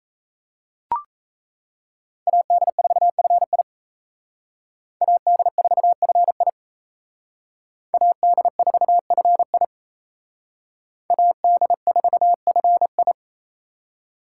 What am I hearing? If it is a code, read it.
Morse code, heard as a keyed tone, about 32 wpm, 700 Hz. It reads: AD4FI AD4FI AD4FI AD4FI
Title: Speed Racing - USA Call Signs 28wpm